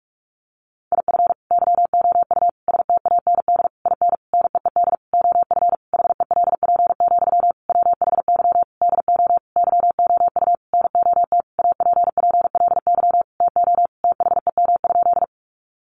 Morse code, heard as a keyed tone, about 44 wpm, 700 Hz. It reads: IF YOU STAND IN NEED OF HELP, WHY DO YOU NOT APPLY TO THEM?
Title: Sentences from Top 1200 Words 45wpm